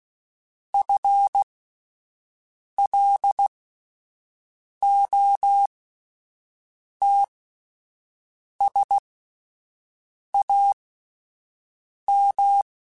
Morse code, heard as a keyed tone, about 16 wpm, spaced as for 5 wpm, 800 Hz. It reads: FLOTSAM
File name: Misc / Misc-mp3s / Word Pairs @5